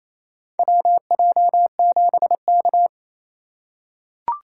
Morse code, heard as a keyed tone, about 28 wpm, 700 Hz. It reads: WJ7K
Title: USA Call Signs - 3x Character Spacing - 28wpm